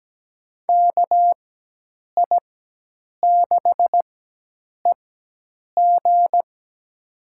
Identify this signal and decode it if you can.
Morse code, heard as a keyed tone, about 17 wpm, spaced as for 8 wpm, 700 Hz. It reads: KI6EG
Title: USA Call Signs - 5 Characters - 4x Character Spacing - 17wpm